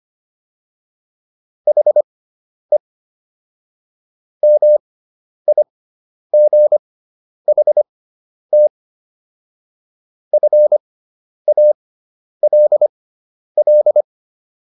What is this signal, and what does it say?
Morse code, heard as a keyed tone, about 25 wpm, spaced as for 10 wpm, 600 Hz. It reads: HE MIGHT FALL